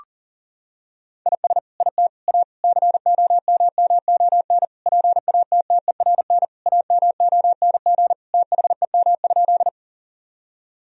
Morse code, heard as a keyed tone, about 40 wpm, 700 Hz. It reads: IS IT A COMMON PATTERN AMONG THEM?
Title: Sentences from Top 1000 Words - Review Part 2 - 40wpm